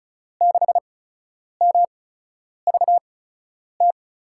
Morse code, heard as a keyed tone, about 34 wpm, spaced as for 10 wpm, 700 Hz. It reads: 6MVT
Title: USA Call Signs - 6 Characters - 8x Character Spacing - 35wpm